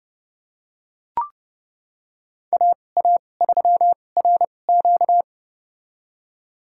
Morse code, heard as a keyed tone, about 30 wpm, spaced as for 22 wpm, 700 Hz. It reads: AA3RQ
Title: USA Call Signs - 2x Character Spacing - 30wpm